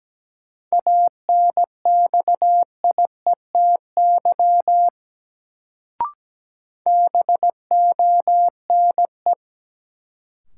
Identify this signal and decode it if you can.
Morse code, heard as a keyed tone, about 17 wpm, 700 Hz. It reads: ANXIETY BONE
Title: Sentences from Top 1600 Words - New Word Drill - 17wpm